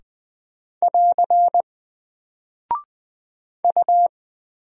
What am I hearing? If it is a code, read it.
Morse code, heard as a keyed tone, about 20 wpm, 700 Hz. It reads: <AR> U
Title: Single Character (Letter:Number:Punctuation:Prosign) 20wpm